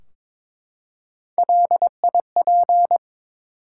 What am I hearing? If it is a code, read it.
Morse code, heard as a keyed tone, about 23 wpm, 700 Hz. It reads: LIP
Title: Three Letter Words - 22wpm